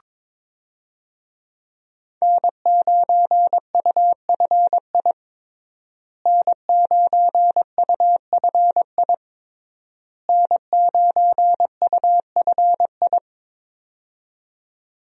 Morse code, heard as a keyed tone, about 22 wpm, 700 Hz. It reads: N9UFI N9UFI N9UFI
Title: USA Call Signs - 5 Characters - Repeated 3x - 22wpm